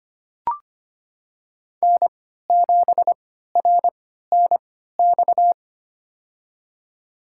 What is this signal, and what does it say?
Morse code, heard as a keyed tone, about 25 wpm, spaced as for 14 wpm, 700 Hz. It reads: N7RNX
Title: USA Call Signs - 3x Character Spacing - 25wpm